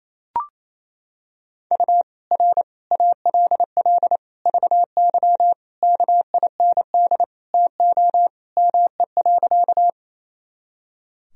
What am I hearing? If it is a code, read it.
Morse code, heard as a keyed tone, about 28 wpm, 700 Hz. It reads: U R ALL VY KIND TO ME.
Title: Sentences from Top 200 Words and Abbreviations 28wpm